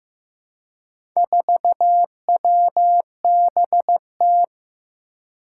Morse code, heard as a keyed tone, about 15 wpm, 700 Hz. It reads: 4WBT
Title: Four Challenging Letter-Number Combinations 15wpm